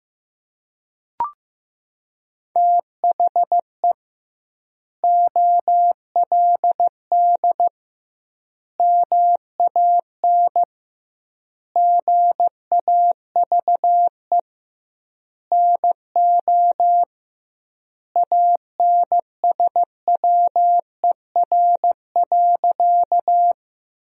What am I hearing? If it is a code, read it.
Morse code, heard as a keyed tone, about 15 wpm, 700 Hz. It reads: THE OLD MAN GAVE NO ANSWER.
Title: 2x Word-Spacing - Sentences from Top 500 Words - 15wpm